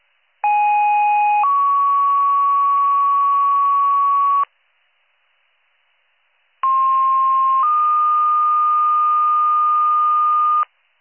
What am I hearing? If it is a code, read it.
Two-tone page, heard as a one-second tone, then a three-second tone, about 830 then 1120 Hz, about 1060 then 1220 Hz. It is heard from 0.4 to 10.6 s.